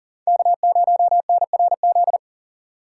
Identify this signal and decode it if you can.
Morse code, heard as a keyed tone, about 52 wpm, 700 Hz. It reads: K0NRZ